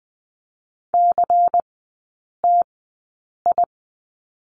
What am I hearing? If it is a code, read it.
Morse code, heard as a keyed tone, about 20 wpm, spaced as for 8 wpm, 700 Hz. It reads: CTI
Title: 093 Lesson 16e - Letter c - Three Characters 20wpm